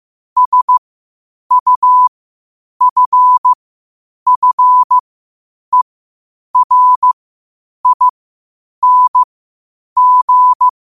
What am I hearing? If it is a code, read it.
Morse code, heard as a keyed tone, about 15 wpm, spaced as for 8 wpm, 1000 Hz. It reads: SUFFERING
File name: qotdfiles / 8wpm